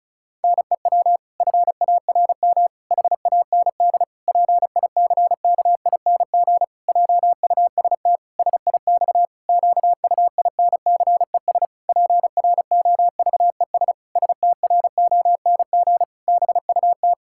Morse code, heard as a keyed tone, about 36 wpm, 700 Hz. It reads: NEW FARM HAND PICKING JUST SIX QUINCES PROVES STRONG BUT
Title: Pangram Challenge v2 - 35wpm